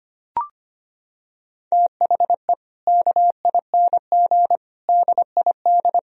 Morse code, heard as a keyed tone, about 25 wpm, 700 Hz. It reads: THE KING DID
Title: Sentences from Top 1000 Words - Review Part 2 - 25wpm